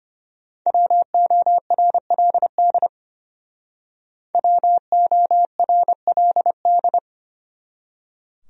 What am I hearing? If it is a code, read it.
Morse code, heard as a keyed tone, about 31 wpm, 700 Hz. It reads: WORLD WORLD
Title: Speed-Racing Top 500 Words 25wpm